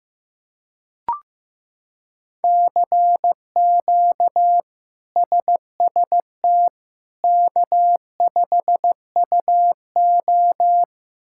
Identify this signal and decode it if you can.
Morse code, heard as a keyed tone, about 15 wpm, 700 Hz. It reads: CQ SST K5UO
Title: K1USN SST QSO Practice - 15wpm